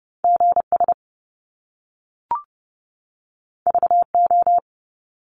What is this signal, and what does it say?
Morse code, heard as a keyed tone, about 30 wpm, 700 Hz. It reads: GS VO
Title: Top Two Letter Combinations in Words 30wpm